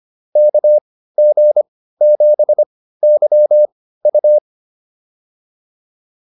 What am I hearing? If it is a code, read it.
Morse code, heard as a keyed tone, about 25 wpm, spaced as for 15 wpm, 590 Hz. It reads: KG7YU